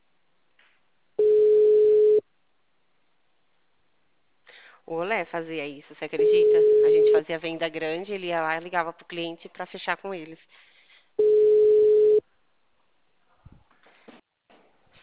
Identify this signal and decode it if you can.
Ringback tone (Europe, 425 Hz), with speech.